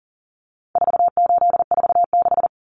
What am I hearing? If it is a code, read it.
Morse code, heard as a keyed tone, about 40 wpm, 700 Hz. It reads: 4846
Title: Common Cut Numbers - 4-Digit 40wpm